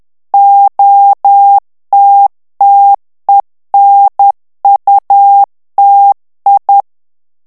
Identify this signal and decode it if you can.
Morse code, heard as a keyed tone, about 10 wpm, 800 Hz. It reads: OTTENUTI
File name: ITALIEN texte clair 2 vitesse 10 mots minute